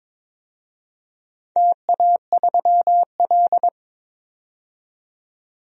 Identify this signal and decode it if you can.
Morse code, heard as a keyed tone, about 22 wpm, 700 Hz. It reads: TA3L
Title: DXCC-100 Call Signs - 4 Characters - 22wpm